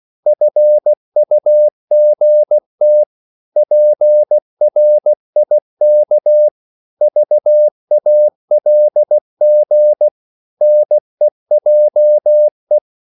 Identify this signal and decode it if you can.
Morse code, heard as a keyed tone, about 16 wpm, 600 Hz. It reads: FUGT PRIK VALG NEJE
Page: Korte danske ord 16wpm